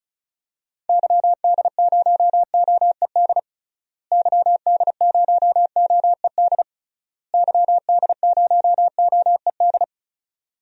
Morse code, heard as a keyed tone, about 36 wpm, 700 Hz. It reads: YD0OED YD0OED YD0OED
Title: DXCC-100 Call Signs - 6 Characters - Repeated 3x - 35wpm